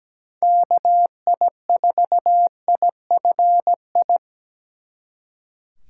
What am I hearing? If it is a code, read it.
Morse code, heard as a keyed tone, about 20 wpm, 700 Hz. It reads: KI4IFI
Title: Speed Racing - USA 2x3 Call Signs 17wpm